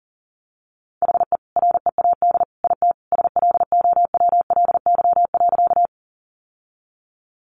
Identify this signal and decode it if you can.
Morse code, heard as a keyed tone, about 40 wpm, 700 Hz. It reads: HE READ IT SLOWLY.
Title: Sentences from Top 1100 Words 40wpm